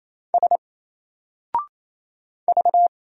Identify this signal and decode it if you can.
Morse code, heard as a keyed tone, about 27 wpm, 700 Hz. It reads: S V
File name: Single-Letter-Number-28wpm